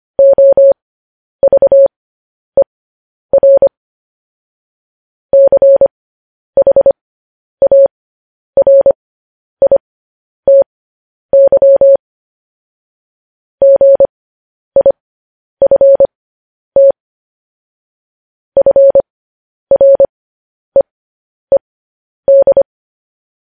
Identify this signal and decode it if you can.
Morse code, heard as a keyed tone, about 25 wpm, spaced as for 10 wpm, 550 Hz. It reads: OVER CHARITY GIFT FREED
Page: news headlines in morse code at 10 wpm